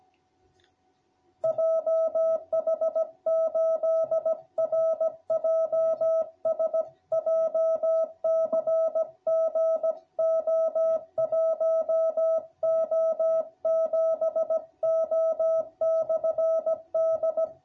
Morse code, heard as a keyed tone, about 16 wpm, 660 Hz. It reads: JH8RJS JCG O1O7O/D